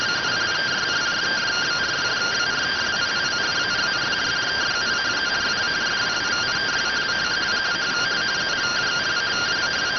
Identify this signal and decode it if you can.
Two-tone teleprinter signal, about 170 Hz shift.